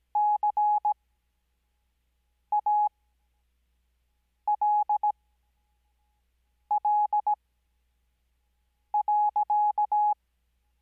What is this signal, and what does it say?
Morse code, heard as a keyed tone, about 17 wpm, spaced as for 5 wpm, 840 Hz. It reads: CALL.
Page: Farnsworth Spacing: 15WPM Character speed Spaced at 5WPM.